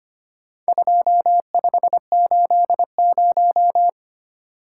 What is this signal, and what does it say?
Morse code, heard as a keyed tone, about 25 wpm, 700 Hz. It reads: 2580
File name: Four Letter-Number Combinations - 25wpm